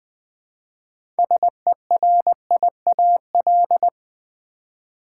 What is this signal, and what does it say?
Morse code, heard as a keyed tone, about 20 wpm, 700 Hz. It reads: SERIAL
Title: 6 Letter Words - First 3 Letters Given - 20wpm